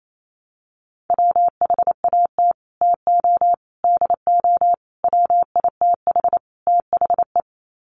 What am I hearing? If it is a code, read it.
Morse code, heard as a keyed tone, about 28 wpm, 700 Hz. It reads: WHAT TO DO WITH THE
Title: Sets of 5 Words Encore2 28wpm